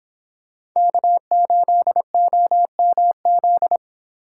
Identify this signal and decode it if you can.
Morse code, heard as a keyed tone, about 24 wpm, 700 Hz. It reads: K8OMZ